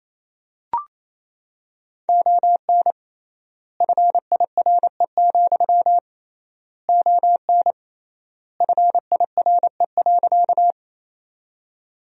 Morse code, heard as a keyed tone, about 28 wpm, 700 Hz. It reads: ON FIRE, ON FIRE.
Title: 3x Word-Spacing - Sentences from Top 400 Words - 28wpm